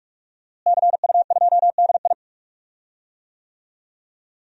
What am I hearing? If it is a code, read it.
Morse code, heard as a keyed tone, about 41 wpm, 700 Hz. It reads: CU2DI
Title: DXCC-100 Call Signs - 5 Characters - Repeated 2x - 45wpm